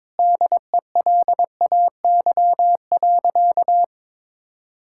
Morse code, heard as a keyed tone, about 22 wpm, 700 Hz. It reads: DELAY.
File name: practice / 22wpm / Sentences with 2-Word Phrases - 22wpm